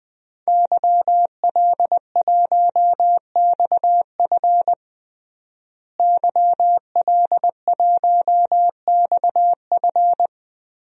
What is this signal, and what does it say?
Morse code, heard as a keyed tone, about 20 wpm, 700 Hz. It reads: YL1XF YL1XF
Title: DXCC-100 Call Signs - 5 Characters - Repeated 3x - 20wpm